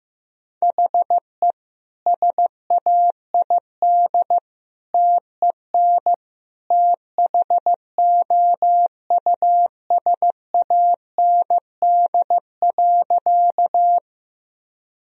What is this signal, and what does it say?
Morse code, heard as a keyed tone, about 16 wpm, 700 Hz. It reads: HE SAID TEN THOUSAND.
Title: Sentences from Top 1000 Words - Review Part 3 - 15wpm